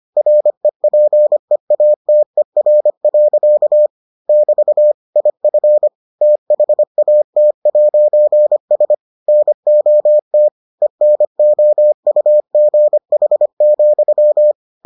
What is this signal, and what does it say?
Morse code, heard as a keyed tone, about 25 wpm, 600 Hz. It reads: REPEATER. <BT> IF THAT'S NOT ENOUGH,